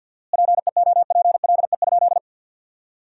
Morse code, heard as a keyed tone, about 51 wpm, 700 Hz. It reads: PEOPLE?
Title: 4x Word-Spacing - Sentences from Top 200 Words - 50wpm